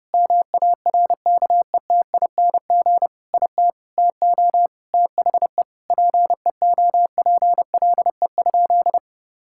Morse code, heard as a keyed tone, about 30 wpm, 700 Hz. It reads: MARKETING IT TO THE PEOPLE?
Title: Sentences from Top 1000 Words - Review Part 5 - 30wpm